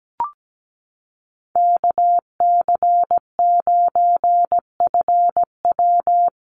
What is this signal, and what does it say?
Morse code, heard as a keyed tone, about 17 wpm, 700 Hz. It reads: KC9FW
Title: USA Call Signs 17wpm